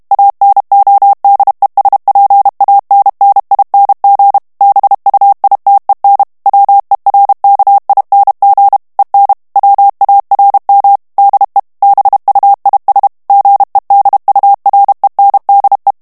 Morse code, heard as a keyed tone, about 32 wpm, 800 Hz. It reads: ANODESPANNING BUITEN WERKING EN WARM DE BUIS GEDURENDE
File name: HOLLANDAIS texte clair 2 vitesse 30 mots minute